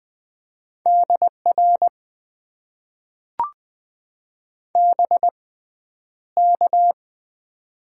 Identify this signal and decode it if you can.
Morse code, heard as a keyed tone, about 20 wpm, spaced as for 10 wpm, 700 Hz. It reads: DR BK
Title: Top 90 QSO Elements - 6x Character Spacing - 20wpm